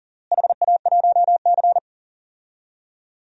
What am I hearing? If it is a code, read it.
Morse code, heard as a keyed tone, about 37 wpm, 700 Hz. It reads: HA1C
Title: DXCC-100 Call Signs - 4 Characters - Repeated 3x - 40wpm